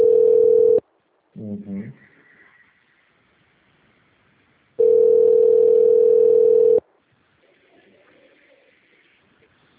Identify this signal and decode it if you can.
Ringback tone (North America), 440 + 480 Hz, with speech.
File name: out- - - - - .55376